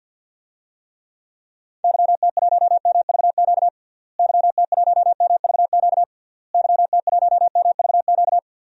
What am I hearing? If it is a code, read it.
Morse code, heard as a keyed tone, about 49 wpm, 700 Hz. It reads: YT1MVQ YT1MVQ YT1MVQ
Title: DXCC-100 Call Signs - 6 Characters - Repeated 3x - 50wpm